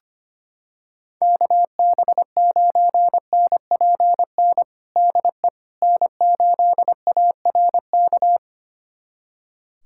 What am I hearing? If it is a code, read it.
Morse code, heard as a keyed tone, about 25 wpm, 700 Hz. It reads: KB9NPN DE N8ARK
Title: USA Callsign DE Callsign 25wpm